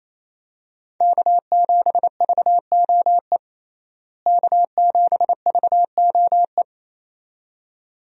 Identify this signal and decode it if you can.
Morse code, heard as a keyed tone, about 28 wpm, 700 Hz. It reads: K7VOE K7VOE
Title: USA Call Signs - 5 Characters - Repeated 2x - 28wpm